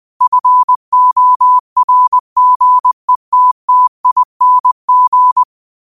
Morse code, heard as a keyed tone, about 20 wpm, 1000 Hz. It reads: FORGETTING